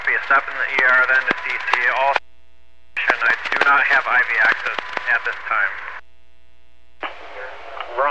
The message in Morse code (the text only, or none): none